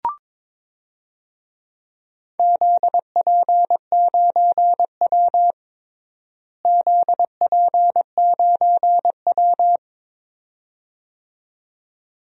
Morse code, heard as a keyed tone, about 22 wpm, 700 Hz. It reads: ZP9W ZP9W
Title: DXCC-100 Call Signs - 4 Characters - Repeated 2x - 22wpm